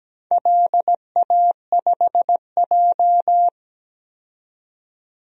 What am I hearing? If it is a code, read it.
Morse code, heard as a keyed tone, about 17 wpm, 700 Hz. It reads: LA5J
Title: DXCC-100 Call Signs - 4 Characters - 17wpm